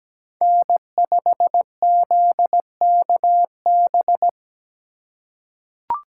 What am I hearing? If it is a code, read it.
Morse code, heard as a keyed tone, about 17 wpm, 700 Hz. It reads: N5ZKB
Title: USA Call Signs - 2x Character Spacing - 17wpm